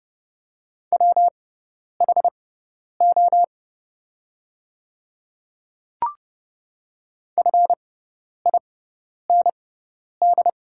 Morse code, heard as a keyed tone, about 30 wpm, spaced as for 10 wpm, 690 Hz. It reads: WHO FIND
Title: 6x Character Spacing - Top 300 Words - 30wpm